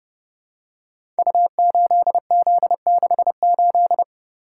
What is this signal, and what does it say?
Morse code, heard as a keyed tone, about 30 wpm, 700 Hz. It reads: U8Z68
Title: Code Groups - 5 Letter:Numbers - 30wpm